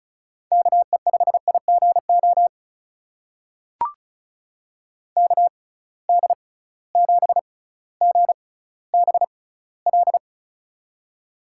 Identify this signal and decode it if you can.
Morse code, heard as a keyed tone, about 36 wpm, spaced as for 18 wpm, 700 Hz. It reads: KE5IGO KD7GBL
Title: USA Call Signs - 6 Characters - 6x Character Spacing - 35wpm